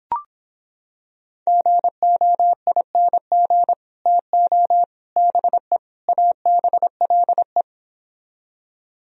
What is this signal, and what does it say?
Morse code, heard as a keyed tone, about 26 wpm, 700 Hz. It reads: GOING TO BE ABLE